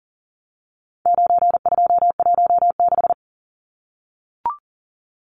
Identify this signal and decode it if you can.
Morse code, heard as a keyed tone, about 40 wpm, 700 Hz. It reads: 9216
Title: Numbers - 4-Digit 40wpm